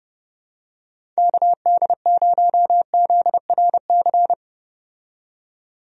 Morse code, heard as a keyed tone, about 29 wpm, 700 Hz. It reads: KD0ZRC
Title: USA Call Signs - 6 Characters - 30wpm